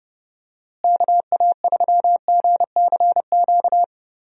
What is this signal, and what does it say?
Morse code, heard as a keyed tone, about 31 wpm, 700 Hz. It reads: KA3GCQ
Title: USA Call Signs - 6 Characters - Repeated 3x - 30wpm